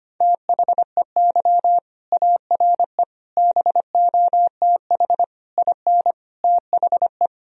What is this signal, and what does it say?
Morse code, heard as a keyed tone, about 25 wpm, 700 Hz. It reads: THEY ARE BOTH IN THE
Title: Sentences from Top 1000 Words - Review Part 1 - 25wpm